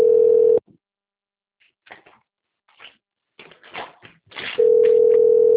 Ringback tone (North America), 440 + 480 Hz.